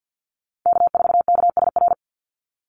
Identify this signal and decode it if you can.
Morse code, heard as a keyed tone, about 49 wpm, 700 Hz. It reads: K4KSR
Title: Call Signs Repeated 3x - 50wpm